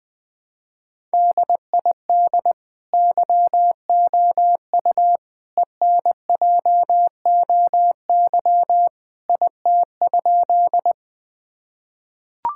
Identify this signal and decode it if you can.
Morse code, heard as a keyed tone, about 21 wpm, 700 Hz. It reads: DID YOU ENJOY IT?
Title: Sentences from Top 1000 Words - Review Part 5 - 20wpm